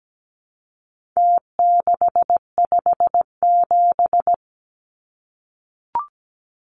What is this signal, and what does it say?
Morse code, heard as a keyed tone, about 17 wpm, 700 Hz. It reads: T657
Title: Numbers - Contest Style - 4-Digit 17wpm